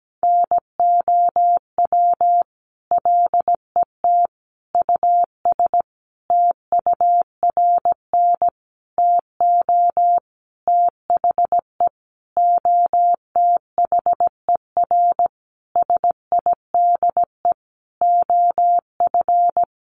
Morse code, heard as a keyed tone, about 17 wpm, 700 Hz. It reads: NOW LET US TURN TO THE OTHER SIDE OF
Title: Sentences from Top 1400 Words 17wpm